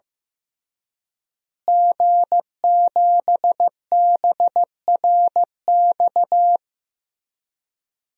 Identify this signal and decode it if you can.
Morse code, heard as a keyed tone, about 15 wpm, 700 Hz. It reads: G7BRX
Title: DXCC-100 Call Signs - 5 Characters - Repeated 2x - 15wpm